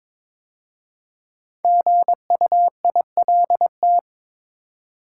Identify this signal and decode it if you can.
Morse code, heard as a keyed tone, about 22 wpm, 700 Hz. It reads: GUILT
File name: 5 Letter Words - First 3 Letters Given - 22wpm